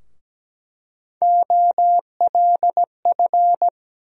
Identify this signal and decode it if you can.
Morse code, heard as a keyed tone, about 17 wpm, 700 Hz. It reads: OLF
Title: USA Call Sign Postfixes - 3 Letters 17wpm